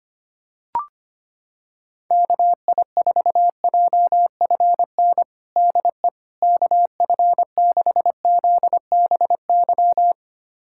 Morse code, heard as a keyed tone, about 25 wpm, 700 Hz. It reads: KI4JFN DE KF6ZBY
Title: USA Callsign DE Callsign - 6 Characters - 25wpm